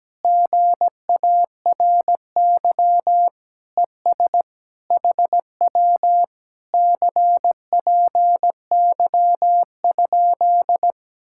Morse code, heard as a keyed tone, about 17 wpm, 700 Hz. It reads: GARY ES HW CPY?
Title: Speed-Racing Common QSO Phrases - 17wpm